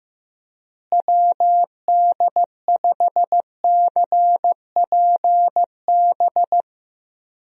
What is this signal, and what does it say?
Morse code, heard as a keyed tone, about 16 wpm, 700 Hz. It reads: WD5CPB
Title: USA Call Signs - 6 Characters - Repeated 2x - 15wpm